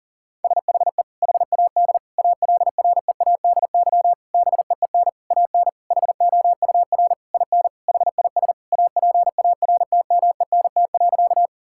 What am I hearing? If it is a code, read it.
Morse code, heard as a keyed tone, about 40 wpm, 700 Hz. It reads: SHE HAD ALREADY BEEN AN HOUR IN HIS APARTMENT.
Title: Sentences from Top 1000 Words - Review Part 6 - 40wpm